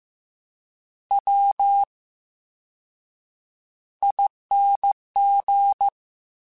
Morse code, heard as a keyed tone, about 15 wpm, spaced as for 7 wpm, 800 Hz. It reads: WING
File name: Special INGs / Special INGs-mp3s / Slow ing @13